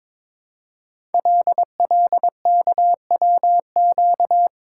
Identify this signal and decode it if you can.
Morse code, heard as a keyed tone, about 22 wpm, 700 Hz. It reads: LLKWQ
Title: Code Groups - 5 Letter:Numbers - 22wpm